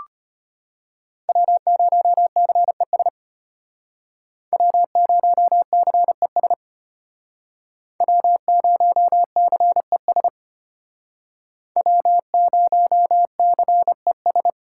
Morse code, heard as a keyed tone, about 33 wpm, 700 Hz. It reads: W0CES W0CES W0CES W0CES